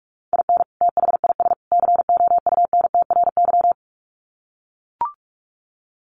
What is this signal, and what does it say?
Morse code, heard as a keyed tone, about 44 wpm, 700 Hz. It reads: IN THIS COUNTRY
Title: Sets of 3 Words 45wpm